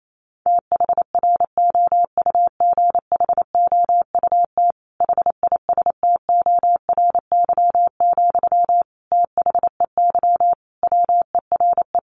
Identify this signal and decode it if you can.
Morse code, heard as a keyed tone, about 28 wpm, 700 Hz. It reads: THROUGHOUT HISTORY, THEY WERE
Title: Sentences from Top 800 Words 28wpm